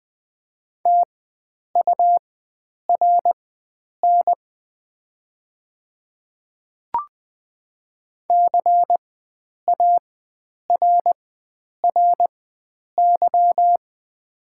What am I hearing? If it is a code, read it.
Morse code, heard as a keyed tone, about 20 wpm, spaced as for 9 wpm, 690 Hz. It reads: TURN CARRY
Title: 4x Character Spacing - Top 300 Words - 20wpm